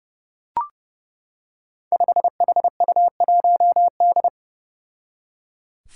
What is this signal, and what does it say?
Morse code, heard as a keyed tone, about 30 wpm, 700 Hz. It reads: 5HU1D
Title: Five Challenging Letter-Number Combinations 30wpm